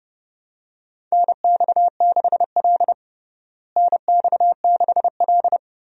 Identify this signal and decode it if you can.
Morse code, heard as a keyed tone, about 32 wpm, 700 Hz. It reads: NX6L NX6L
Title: USA Call Signs - 4 Characters - Repeated 2x - 30wpm